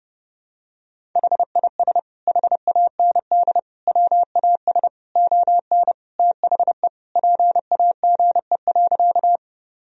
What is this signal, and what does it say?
Morse code, heard as a keyed tone, about 32 wpm, 700 Hz. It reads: HIS HAND WAS ON THE PAGE.
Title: Sentences from Top 1000 Words - Review Part 1 - 30wpm